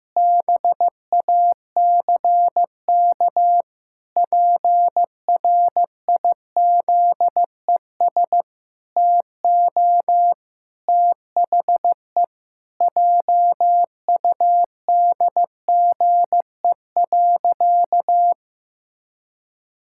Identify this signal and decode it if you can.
Morse code, heard as a keyed tone, about 15 wpm, 700 Hz. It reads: BACK PRIZES TO THE JUDGE.
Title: Pangram Challenge v2 - 15wpm